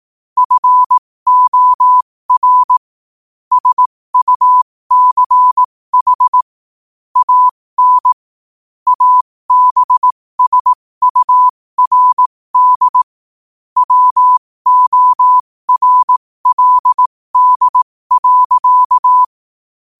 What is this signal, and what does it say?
Morse code, heard as a keyed tone, about 18 wpm, spaced as for 15 wpm, 1000 Hz. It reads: FOR SUCH AN ABSURD WORLD.
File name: qotdfiles / 15wpm